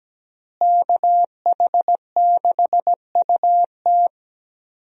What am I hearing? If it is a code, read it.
Morse code, heard as a keyed tone, about 17 wpm, 700 Hz. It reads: KH6UT
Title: USA Call Signs - 5 Characters - Repeated 2x - 17wpm